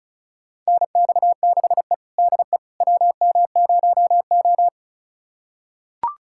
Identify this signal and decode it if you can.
Morse code, heard as a keyed tone, about 36 wpm, 700 Hz. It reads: NX6E DE WM0O
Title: USA Callsign DE Callsign - 4 Characters - 35wpm